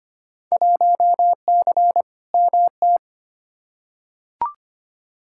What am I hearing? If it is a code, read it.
Morse code, heard as a keyed tone, about 25 wpm, 700 Hz. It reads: 1C MT
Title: Field Day Exchanges Encore - 25wpm